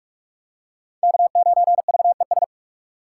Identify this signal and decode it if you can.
Morse code, heard as a keyed tone, about 46 wpm, 700 Hz. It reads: K9VES
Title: USA Call Signs - 5 Characters - 2x Character Spacing - 45wpm